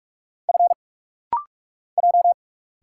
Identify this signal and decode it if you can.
Morse code, heard as a keyed tone, about 45 wpm, 700 Hz. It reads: F J
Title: Single Letter-Number Alt - Rapid-Fire 45wpm